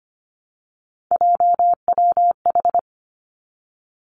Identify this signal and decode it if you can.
Morse code, heard as a keyed tone, about 25 wpm, 700 Hz. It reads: JWH
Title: Three Challenging Letter-Number Combinations 25wpm